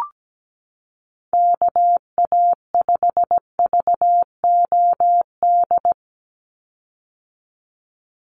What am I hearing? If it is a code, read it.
Morse code, heard as a keyed tone, about 17 wpm, 700 Hz. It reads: KA5VOD
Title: USA Call Signs 17wpm